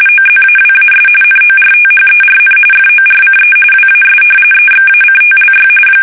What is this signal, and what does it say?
RTTY, about 45.45 baud. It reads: OZAWAS NORTHERN FORCE WAS EXACTLY
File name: fsk850 / battleshipnj